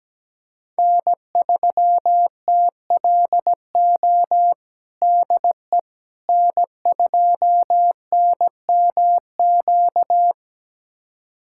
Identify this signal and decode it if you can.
Morse code, heard as a keyed tone, about 17 wpm, 700 Hz. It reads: N3TLO DE N2NMQ
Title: USA Callsign DE Callsign - 5 Characters - 17wpm